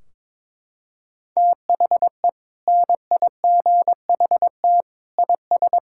Morse code, heard as a keyed tone, about 22 wpm, 700 Hz. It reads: THE NIGHT IS
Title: Sentences from Top 1600 Words 22wpm